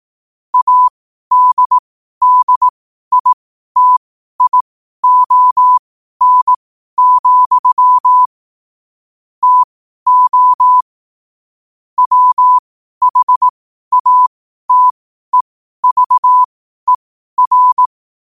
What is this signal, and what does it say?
Morse code, heard as a keyed tone, about 18 wpm, spaced as for 12 wpm, 1000 Hz. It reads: ADDITION, TO WHATEVER
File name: qotdfiles / 12wpm